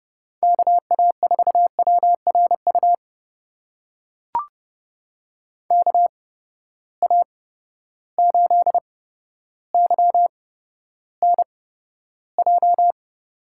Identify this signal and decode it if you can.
Morse code, heard as a keyed tone, about 30 wpm, spaced as for 13 wpm, 700 Hz. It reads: KA4WRU KA8YNJ
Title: USA Call Signs - 6 Characters - 8x Character Spacing - 30wpm